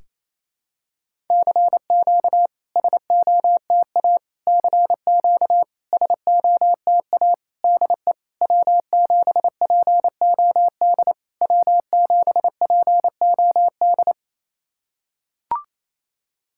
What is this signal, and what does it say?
Morse code, heard as a keyed tone, about 28 wpm, 690 Hz. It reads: CQ SOTA CQ SOTA DE W7POD W7POD
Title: SOTA QSO Practice 28wpm